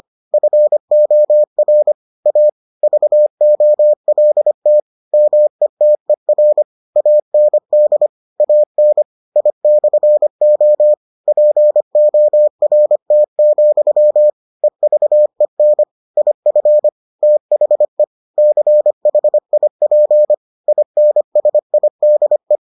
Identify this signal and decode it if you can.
Morse code, heard as a keyed tone, about 25 wpm, 600 Hz. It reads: FOR A VOLT METER AND AN I/O PORT, EVEN IF THE CHIP INSIDE